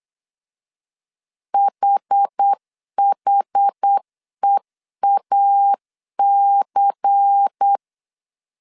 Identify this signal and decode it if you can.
Morse code, heard as a keyed tone, about 8 wpm, 800 Hz. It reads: HHEAC
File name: CW / DET / 1 6 1500C S C 8wpm 5219